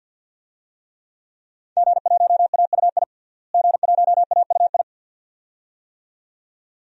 Morse code, heard as a keyed tone, about 47 wpm, 700 Hz. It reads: G1AUI G1AUI
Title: DXCC-100 Call Signs - 5 Characters - Repeated 2x - 50wpm